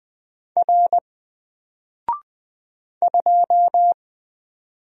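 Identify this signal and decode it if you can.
Morse code, heard as a keyed tone, about 20 wpm, 700 Hz. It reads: R 2
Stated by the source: Single letter and number Morse drill at 20 wpm, from N4LCD.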